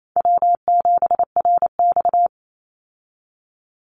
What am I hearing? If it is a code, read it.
Morse code, heard as a keyed tone, about 28 wpm, 700 Hz. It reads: W7RX
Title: Call Signs Repeated 2x - 28wpm